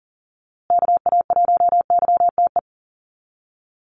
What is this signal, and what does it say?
Morse code, heard as a keyed tone, about 40 wpm, 700 Hz. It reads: KA1YTE
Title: USA Call Signs Repeated 3x 40wpm